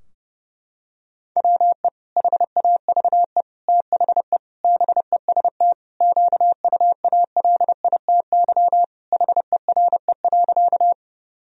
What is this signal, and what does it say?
Morse code, heard as a keyed tone, about 30 wpm, 700 Hz. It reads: WE HAVE THE BEST QUALITY HERE.
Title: Sentences from Top 1000 Words - Review Part 4 - 30wpm